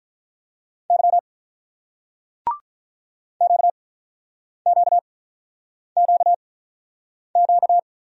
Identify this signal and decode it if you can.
Morse code, heard as a keyed tone, about 48 wpm, spaced as for 9 wpm, 700 Hz. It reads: Y QQQQ